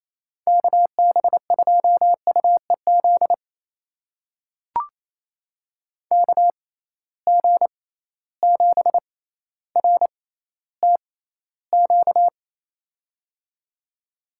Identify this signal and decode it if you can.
Morse code, heard as a keyed tone, about 28 wpm, spaced as for 14 wpm, 700 Hz. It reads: KB2UEZ KG7RTQ
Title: USA Call Signs - 6 Characters - 6x Character Spacing - 28wpm